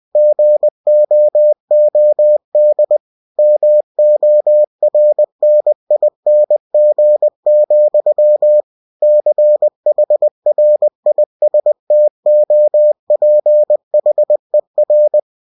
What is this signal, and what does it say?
Morse code, heard as a keyed tone, about 20 wpm, 600 Hz. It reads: GOOD MORNING, CHRISTOPHER
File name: cw-books / winnie-the-pooh / winnie-20wpm-page-0004